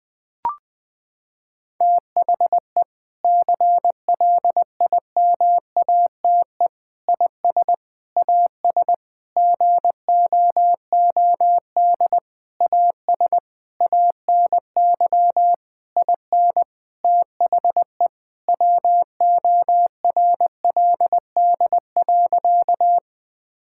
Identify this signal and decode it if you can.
Morse code, heard as a keyed tone, about 20 wpm, 700 Hz. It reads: THE CLIMATE IS AS GOOD AS ANY IN THE WORLD.
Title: Sentences from Top 1600 Words 20wpm